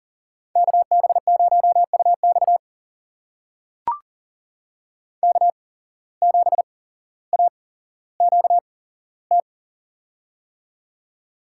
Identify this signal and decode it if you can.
Morse code, heard as a keyed tone, about 40 wpm, spaced as for 17 wpm, 700 Hz. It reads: KB0UX K7AQT
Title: USA Call Signs - 5 Characters - 8x Character Spacing - 40wpm